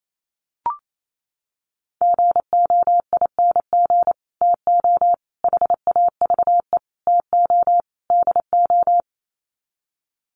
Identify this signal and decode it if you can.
Morse code, heard as a keyed tone, about 28 wpm, 700 Hz. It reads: GOING TO HAVE TO DO
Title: Sets of 5 Words Encore 28wpm